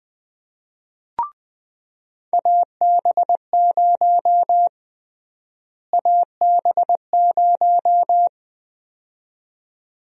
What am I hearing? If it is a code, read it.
Morse code, heard as a keyed tone, about 20 wpm, 700 Hz. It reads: AB0 AB0
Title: USA Call Sign Prefixes - Repeated 2x - 20wpm